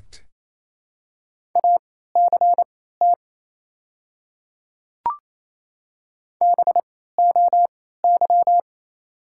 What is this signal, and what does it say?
Morse code, heard as a keyed tone, about 28 wpm, spaced as for 16 wpm, 690 Hz. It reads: ACT BOY
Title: 3x Character Spacing - Top 200 Words - 28wpm